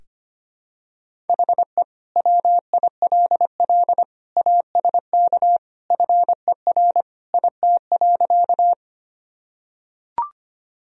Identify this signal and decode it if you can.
Morse code, heard as a keyed tone, about 25 wpm, 700 Hz. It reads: HE WILL ASK FER IT.
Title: Sentences from Top 200 Words and Abbreviations 25wpm